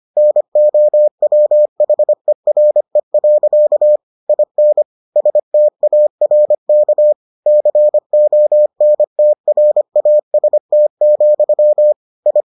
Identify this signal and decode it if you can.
Morse code, heard as a keyed tone, about 25 wpm, 600 Hz. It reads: NOWHERE. IN STARK CONTRAST, I